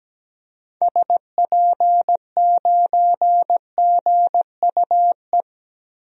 Morse code, heard as a keyed tone, about 17 wpm, 700 Hz. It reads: SP9GUE
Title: DXCC-100 Call Signs - 6 Characters - Repeated 3x - 17wpm